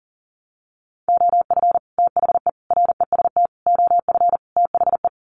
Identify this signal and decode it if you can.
Morse code, heard as a keyed tone, about 40 wpm, 700 Hz. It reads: OF THE REST OF THE
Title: Sets of 5 Words 40wpm